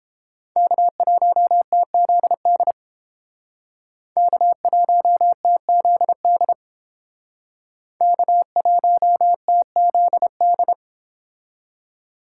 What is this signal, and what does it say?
Morse code, heard as a keyed tone, about 30 wpm, 700 Hz. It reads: K1TZD K1TZD K1TZD